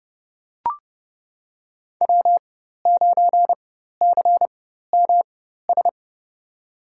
Morse code, heard as a keyed tone, about 30 wpm, spaced as for 14 wpm, 700 Hz. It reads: W9CMS
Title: USA Call Signs - 5 Characters - 4x Character Spacing - 30wpm